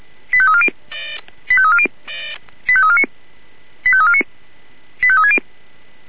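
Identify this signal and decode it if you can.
Five-tone selective call (ZVEI), heard as quick runs of five tones, tones of about 70 ms. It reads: 85340 85340 85349 85349 85360
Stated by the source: ZVEI five-tone selective call received by SDR.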